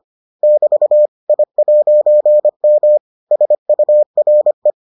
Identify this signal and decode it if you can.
Morse code, heard as a keyed tone, about 25 wpm, 600 Hz. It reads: <BT> I'M SURE